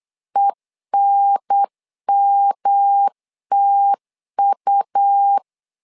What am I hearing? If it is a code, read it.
Morse code, heard as a keyed tone, about 8 wpm, 800 Hz. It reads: ENMTU
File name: LEZIONI / 3 MWRO / 3 M 8wpm 2300